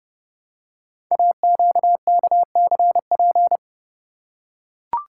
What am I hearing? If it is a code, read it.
Morse code, heard as a keyed tone, about 30 wpm, 700 Hz. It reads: AQKCP
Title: Code Groups - 5 Letters - 30wpm